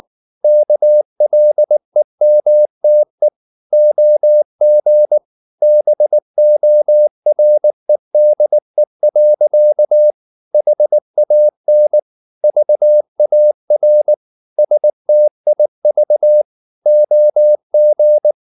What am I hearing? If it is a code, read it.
Morse code, heard as a keyed tone, about 19 wpm, 600 Hz. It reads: KLEMTE OG BOREDE. HAN VAR STIV OG
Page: Ferie 19 wpm